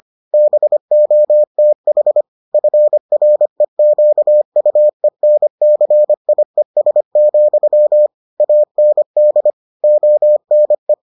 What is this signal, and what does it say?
Morse code, heard as a keyed tone, about 25 wpm, 600 Hz. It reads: BOTH FREQUENCIES, AND ONE